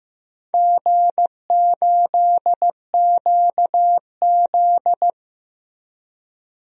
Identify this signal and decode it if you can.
Morse code, heard as a keyed tone, about 15 wpm, 700 Hz. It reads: G8QZ
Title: DXCC-100 Call Signs - 4 Characters - Repeated 2x - 15wpm